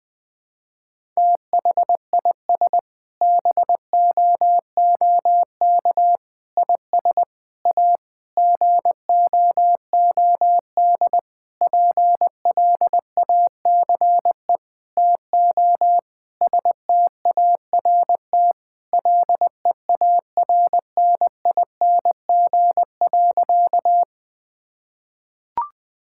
Morse code, heard as a keyed tone, about 21 wpm, 700 Hz. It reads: THIS BOOK IS A GOOD PLACE TO START LEARNING.
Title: Sentences with 6-Word Phrases - 20wpm